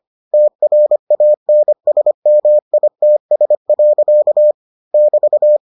Morse code, heard as a keyed tone, about 25 wpm, 600 Hz. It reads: TRANSMITS. <BT>